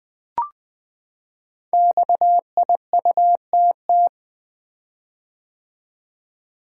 Morse code, heard as a keyed tone, about 20 wpm, 700 Hz. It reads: XIUTT
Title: Code Groups - 5 Letters - 20wpm